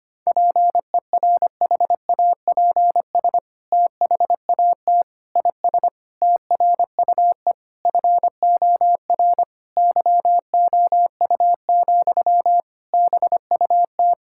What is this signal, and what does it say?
Morse code, heard as a keyed tone, about 25 wpm, 700 Hz. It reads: PERHAPS THAT IS TRUE FOR YOU, BUT
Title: Sentences from Top 1000 Words - Review Part 5 - 25wpm